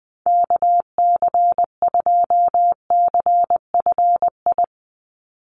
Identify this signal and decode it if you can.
Morse code, heard as a keyed tone, about 20 wpm, 700 Hz. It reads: KC2CFI
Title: USA Call Signs 20wpm